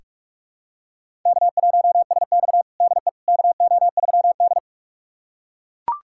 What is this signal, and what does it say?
Morse code, heard as a keyed tone, about 42 wpm, 700 Hz. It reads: K1SX DE KO3D
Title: USA Callsign DE Callsign - 4 Characters - 45wpm